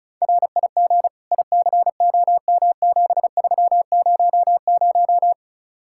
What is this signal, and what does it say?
Morse code, heard as a keyed tone, about 43 wpm, 700 Hz. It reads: RIG ICOM7300
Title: Speed-Racing Common QSO Phrases - 35wpm